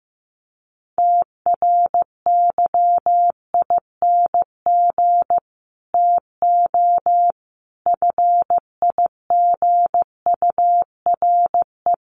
Morse code, heard as a keyed tone, about 15 wpm, 700 Hz. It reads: TRYING TO FIGURE
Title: Sets of 4 Words 15wpm